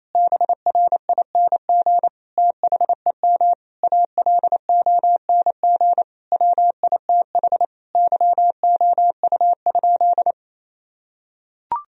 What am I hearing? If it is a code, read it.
Morse code, heard as a keyed tone, about 28 wpm, 700 Hz. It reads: BRING THEM ALONG WITH YOU?
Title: Sentences from Top 1000 Words - Review Part 3 - 28wpm